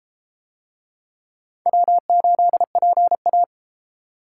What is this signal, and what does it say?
Morse code, heard as a keyed tone, about 32 wpm, 700 Hz. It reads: W8PA